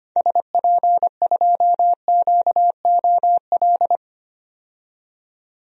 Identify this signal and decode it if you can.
Morse code, heard as a keyed tone, about 25 wpm, 700 Hz. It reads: SP2QOL
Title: DXCC-100 Call Signs - 6 Characters - 25wpm